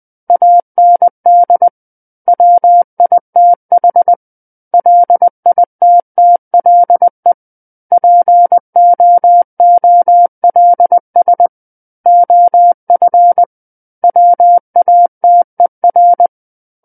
Morse code, heard as a keyed tone, about 20 wpm, 700 Hz. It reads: AND WITH LITTLE POOLS OF WATER
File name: 20 WPM